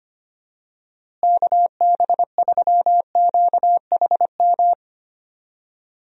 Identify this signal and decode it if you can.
Morse code, heard as a keyed tone, about 25 wpm, 700 Hz. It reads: KB3QHM
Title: USA Call Signs - 6 Characters - Repeated 3x - 25wpm